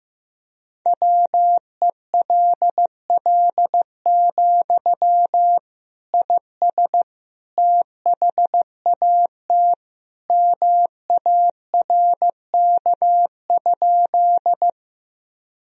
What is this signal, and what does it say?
Morse code, heard as a keyed tone, about 15 wpm, 700 Hz. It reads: WELL, IS THAT MARK?
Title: Sentences from Top 1000 Words - Review Part 1 - 15wpm